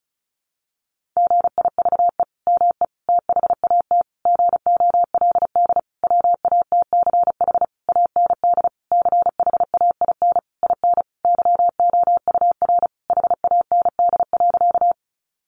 Morse code, heard as a keyed tone, about 34 wpm, 700 Hz. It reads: GIVE ME THAT GOLD WATCH AND CHAIN IN YOUR HAND.
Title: Sentences from Top 1500 Words 35wpm